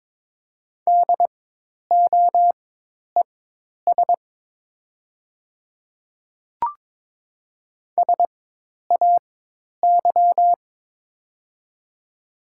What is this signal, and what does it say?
Morse code, heard as a keyed tone, about 22 wpm, spaced as for 10 wpm, 700 Hz. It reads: DOES SAY
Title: 4x Character Spacing - Top 200 Words - 22wpm